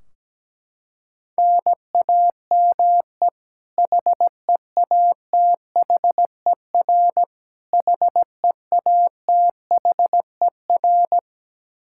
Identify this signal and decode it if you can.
Morse code, heard as a keyed tone, about 17 wpm, 700 Hz. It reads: NAME HEATHER HEATHER
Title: K1USN SST QSO Practice 17wpm